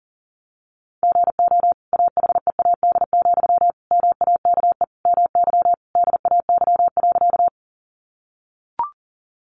Morse code, heard as a keyed tone, about 40 wpm, 700 Hz. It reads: GO AHEAD, MAKE MY DAY.
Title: Sentences from Top 1100 Words 40wpm